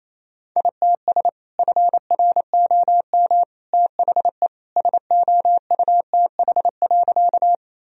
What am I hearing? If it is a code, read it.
Morse code, heard as a keyed tone, about 28 wpm, 700 Hz. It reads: ITS FROM THE SOUTH.
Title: Sentences from Top 1000 Words - Review Part 2 - 28wpm